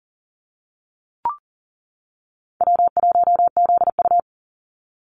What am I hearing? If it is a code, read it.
Morse code, heard as a keyed tone, about 40 wpm, 700 Hz. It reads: W1ZU
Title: NA SOTA Activator Callsigns 40wpm